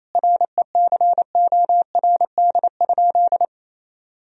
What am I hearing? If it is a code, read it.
Morse code, heard as a keyed tone, about 28 wpm, 700 Hz. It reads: RECORD?
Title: Sentences from Top 1000 Words - Review Part 3 - 28wpm